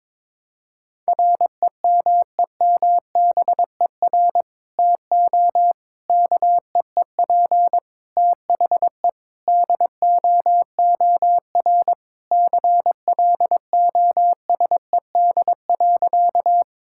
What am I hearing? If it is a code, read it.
Morse code, heard as a keyed tone, about 22 wpm, 700 Hz. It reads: REMEMBER TO KEEP THE DOOR CLOSED.
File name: practice / 22wpm / Sentences with 3-Word Phrases - 22wpm